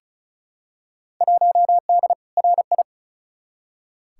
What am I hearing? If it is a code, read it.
Morse code, heard as a keyed tone, about 36 wpm, 700 Hz. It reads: 1D RI
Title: Field Day Exchanges - 35wpm